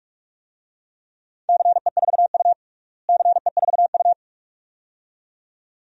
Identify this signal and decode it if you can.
Morse code, heard as a keyed tone, about 46 wpm, 700 Hz. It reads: CE4U CE4U
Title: DXCC-100 Call Signs - 4 Characters - Repeated 2x - 45wpm